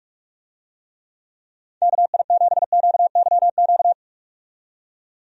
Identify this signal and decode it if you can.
Morse code, heard as a keyed tone, about 47 wpm, 700 Hz. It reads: KI7QYQ
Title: USA Call Signs - 6 Characters - 45wpm